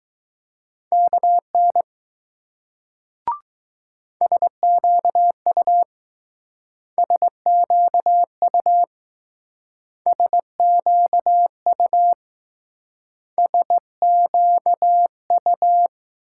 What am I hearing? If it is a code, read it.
Morse code, heard as a keyed tone, about 19 wpm, 700 Hz. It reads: KN SQU SQU SQU SQU